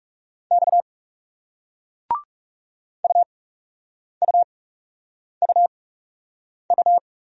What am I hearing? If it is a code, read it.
Morse code, heard as a keyed tone, about 40 wpm, spaced as for 8 wpm, 700 Hz. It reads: X UUUU